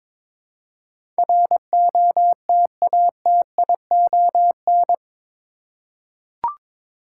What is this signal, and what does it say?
Morse code, heard as a keyed tone, about 22 wpm, 700 Hz. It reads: ROTATION
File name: practice / 22wpm / Eight Letter Words - 22wpm